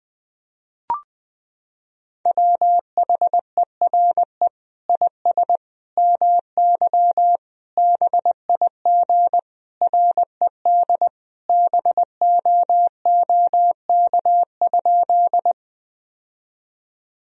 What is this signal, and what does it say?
Morse code, heard as a keyed tone, about 20 wpm, 700 Hz. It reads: WHERE IS MY BIG RED BOOK?
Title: Sentences from Top 1000 Words - Review Part 2 - 20wpm